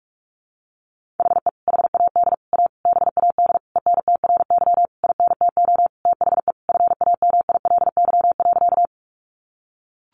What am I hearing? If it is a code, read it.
Morse code, heard as a keyed tone, about 44 wpm, 700 Hz. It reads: HE HAD A BAD ENTRY INTO THE FAMILY.
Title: Sentences from Top 1500 Words 45wpm